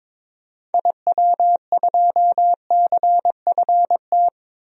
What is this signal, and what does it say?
Morse code, heard as a keyed tone, about 22 wpm, 700 Hz. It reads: IW2CFT
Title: DXCC-100 Call Signs - 6 Characters - 22wpm